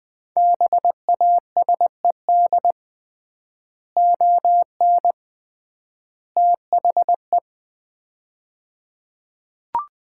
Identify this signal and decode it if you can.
Morse code, heard as a keyed tone, about 20 wpm, 700 Hz. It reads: BASED ON THE
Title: 3x Word-Spacing - Sets of 3 Words - 20wpm